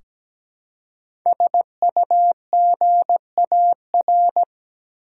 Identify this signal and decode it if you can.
Morse code, heard as a keyed tone, about 17 wpm, 700 Hz. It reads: SUGAR
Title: Five Letter Words - 17wpm